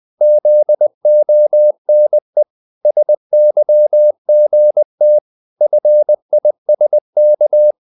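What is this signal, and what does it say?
Morse code, heard as a keyed tone, about 20 wpm, 600 Hz. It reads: ZONE SYGT FISK
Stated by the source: Morse code practice of short Danish words at 20 wpm, from CW med Gnister.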